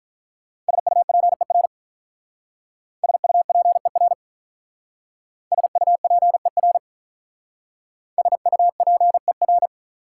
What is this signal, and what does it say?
Morse code, heard as a keyed tone, about 44 wpm, 690 Hz. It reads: SUPER SUPER SUPER SUPER
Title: Speed-Racing Prefixes and Suffixes 35wpm